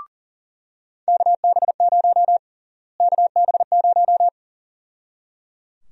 Morse code, heard as a keyed tone, about 37 wpm, 700 Hz. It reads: KB0 KB0
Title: USA Call Sign Prefixes - Repeated 2x - 40wpm